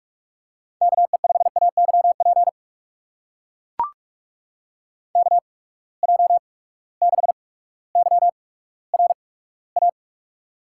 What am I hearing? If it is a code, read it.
Morse code, heard as a keyed tone, about 44 wpm, spaced as for 19 wpm, 700 Hz. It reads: KE5AYP KJ6YRA
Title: USA Call Signs - 6 Characters - 8x Character Spacing - 45wpm